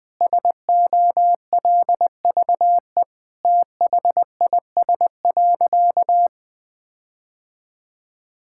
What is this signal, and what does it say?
Morse code, heard as a keyed tone, about 19 wpm, 700 Hz. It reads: SOLVE THIS.
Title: Sentences with 7-Word Phrases - Spoken Beginning - 20wpm